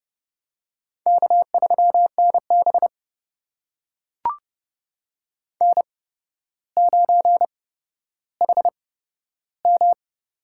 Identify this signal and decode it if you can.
Morse code, heard as a keyed tone, about 30 wpm, spaced as for 13 wpm, 700 Hz. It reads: K3NB N9HM
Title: USA Call Signs - 4 Characters - 8x Character Spacing - 30wpm